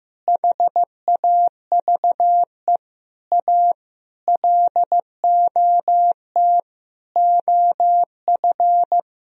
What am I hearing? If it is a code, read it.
Morse code, heard as a keyed tone, about 15 wpm, 700 Hz. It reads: HAVE A LOT OF
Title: Sentences from Top 1000 Words - Review Part 4 - 15wpm